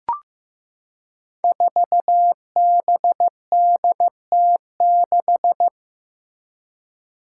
Morse code, heard as a keyed tone, about 15 wpm, 700 Hz. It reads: 4BDT6
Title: Five Challenging Letter-Number Combinations 15wpm